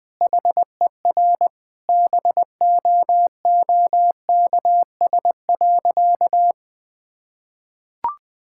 Morse code, heard as a keyed tone, about 20 wpm, 700 Hz. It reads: HER BOOKS.
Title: Sentences with 2-Word Phrases - 20wpm